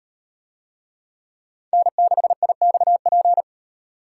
Morse code, heard as a keyed tone, about 38 wpm, 700 Hz. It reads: N6IXP